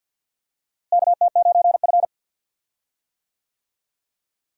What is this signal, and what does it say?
Morse code, heard as a keyed tone, about 46 wpm, 700 Hz. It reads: KT9F
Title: USA Call Signs - 4 Characters - Repeated 2x - 50wpm